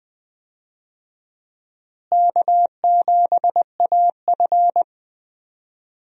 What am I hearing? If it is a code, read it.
Morse code, heard as a keyed tone, about 21 wpm, 700 Hz. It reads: K7AF
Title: USA Call Signs - 4 Characters - Repeated 3x - 20wpm